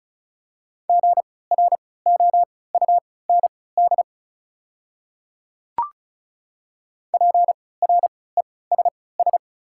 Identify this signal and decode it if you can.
Morse code, heard as a keyed tone, about 36 wpm, spaced as for 20 wpm, 690 Hz. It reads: GROUND PRESS
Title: 3x Character Spacing - Top 500 Words - 35wpm